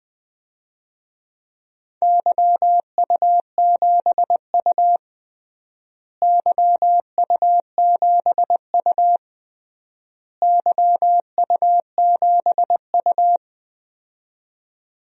Morse code, heard as a keyed tone, about 20 wpm, 700 Hz. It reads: YU7U YU7U YU7U
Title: DXCC-100 Call Signs - 4 Characters - Repeated 3x - 20wpm